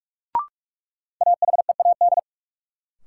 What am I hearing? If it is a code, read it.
Morse code, heard as a keyed tone, about 42 wpm, 700 Hz. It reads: AHEAD
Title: Sentences from Top 1100 Words - New Word Drill Alt - 45wpm